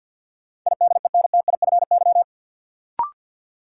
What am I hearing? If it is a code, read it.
Morse code, heard as a keyed tone, about 53 wpm, 700 Hz. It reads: IDENTIFY
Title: Sentences from Top 900 Words - New Word Drill - 50wpm